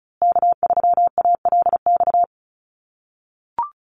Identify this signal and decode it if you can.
Morse code, heard as a keyed tone, about 34 wpm, 700 Hz. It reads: K3ALX
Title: USA Call Signs Repeated 3x 35wpm